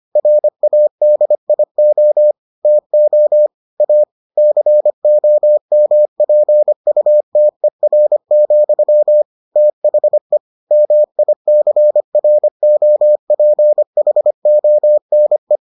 Morse code, heard as a keyed tone, about 25 wpm, 600 Hz. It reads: RADIO TO A COMPUTER, THE MICROPHONE